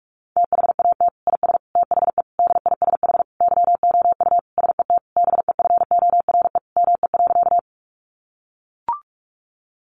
Morse code, heard as a keyed tone, about 44 wpm, 700 Hz. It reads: THAT IS THE DISH YOU SET BEFORE ME.
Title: Sentences from Top 1300 Words 45wpm